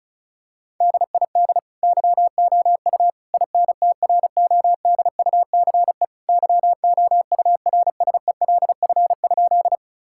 Morse code, heard as a keyed tone, about 34 wpm, 700 Hz. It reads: DID YOU INTRODUCE YOURSELF?
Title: Sentences from Top 1600 Words 35wpm